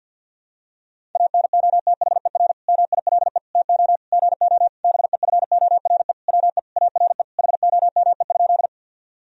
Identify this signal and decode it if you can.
Morse code, heard as a keyed tone, about 50 wpm, 700 Hz. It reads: ANOTHER MILE TO GO BEFORE WE ARE HOME?
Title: Sentences from Top 1000 Words - Review Part 1 - 50wpm